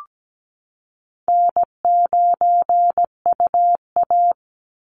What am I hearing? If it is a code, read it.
Morse code, heard as a keyed tone, about 17 wpm, 700 Hz. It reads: N9UA
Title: Easy Call Signs Repeated 2x - 17wpm